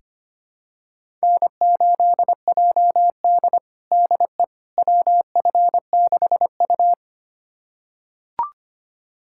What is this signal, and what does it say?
Morse code, heard as a keyed tone, about 25 wpm, 700 Hz. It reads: N8JD DE WF6U
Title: USA Callsign DE Callsign - 4 Characters - 25wpm